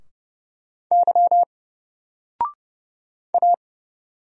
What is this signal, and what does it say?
Morse code, heard as a keyed tone, about 30 wpm, 700 Hz. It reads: Y A
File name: Single-Letter-Number-30wpm